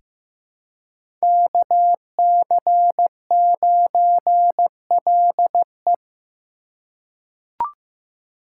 Morse code, heard as a keyed tone, about 15 wpm, 700 Hz. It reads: KC9LE
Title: USA Call Signs - 5 Characters - 3x Character Spacing - 15wpm